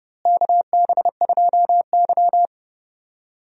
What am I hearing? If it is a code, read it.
Morse code, heard as a keyed tone, about 32 wpm, 700 Hz. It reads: KB2Y
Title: USA Call Signs - 4 Characters - Repeated 3x - 30wpm